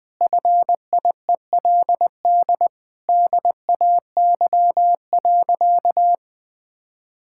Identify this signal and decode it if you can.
Morse code, heard as a keyed tone, about 20 wpm, 700 Hz. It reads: FIELD DAY.
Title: Sentences from Top 1000 Words - Review Part 6 - 20wpm